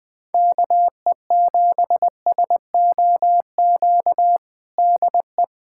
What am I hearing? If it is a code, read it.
Morse code, heard as a keyed tone, about 20 wpm, 700 Hz. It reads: KE7SOQ DE
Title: USA Callsign DE Callsign 20wpm